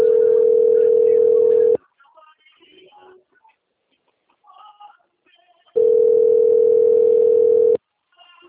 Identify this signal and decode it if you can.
Ringback tone (North America), 440 + 480 Hz.
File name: out- - - - - .14897